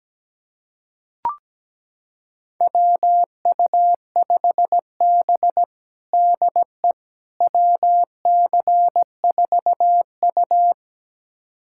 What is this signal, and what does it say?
Morse code, heard as a keyed tone, about 17 wpm, 700 Hz. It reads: WU5B DE WC4U
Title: USA Callsign DE Callsign - 4 Characters - 17wpm